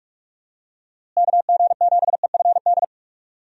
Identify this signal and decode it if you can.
Morse code, heard as a keyed tone, about 46 wpm, 700 Hz. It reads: KG7EFD
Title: USA Call Signs - 6 Characters - Repeated 3x - 45wpm